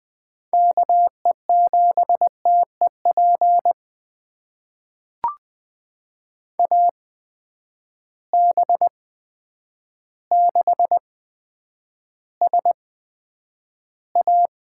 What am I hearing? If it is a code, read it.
Morse code, heard as a keyed tone, about 20 wpm, spaced as for 9 wpm, 700 Hz. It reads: KE7TEP AB6SA
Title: USA Call Signs - 8x Character Spacing - 20wpm